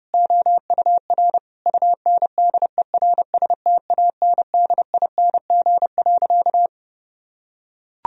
Morse code, heard as a keyed tone, about 30 wpm, 700 Hz. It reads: OUR UNDERSTANDING.
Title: Sentences from Top 1000 Words - Review Part 3 - 30wpm